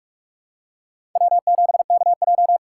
Morse code, heard as a keyed tone, about 46 wpm, 700 Hz. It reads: W7KJ
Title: USA Call Signs - 4 Characters - 4x Character Spacing - 45wpm